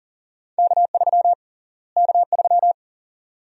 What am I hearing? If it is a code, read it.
Morse code, heard as a keyed tone, about 40 wpm, 700 Hz. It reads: K3 K3
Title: USA Call Sign Prefixes - Repeated 2x - 40wpm